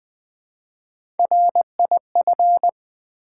Morse code, heard as a keyed tone, about 20 wpm, 700 Hz. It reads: RIF
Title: Build A Word - 1 Letter At A Time - 5-Letter Words - 20wpm